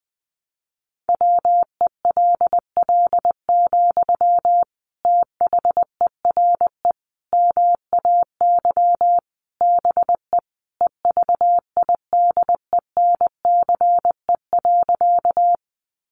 Morse code, heard as a keyed tone, about 20 wpm, 700 Hz. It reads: WELL, THERE MAY BE EVIDENCE.
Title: Sentences from Top 1300 Words 20wpm